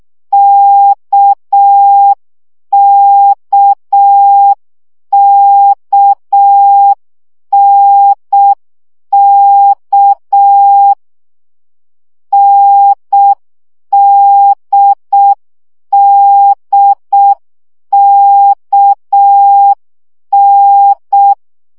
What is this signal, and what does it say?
Morse code, heard as a keyed tone, about 6 wpm, 800 Hz. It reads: KKKNK NDDKN
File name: CW gyakorlas n d k 5 baud